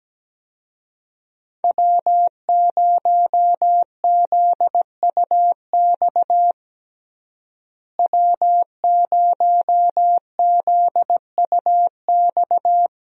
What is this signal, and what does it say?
Morse code, heard as a keyed tone, about 17 wpm, 700 Hz. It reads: W0ZUX W0ZUX
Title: USA Call Signs - 5 Characters - Repeated 2x - 17wpm